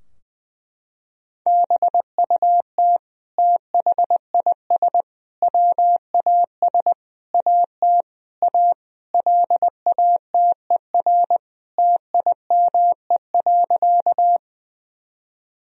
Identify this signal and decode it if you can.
Morse code, heard as a keyed tone, about 20 wpm, 700 Hz. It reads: BUT THIS WAS AT A LATER TIME.
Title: Sentences from Top 1000 Words - Review Part 4 - 20wpm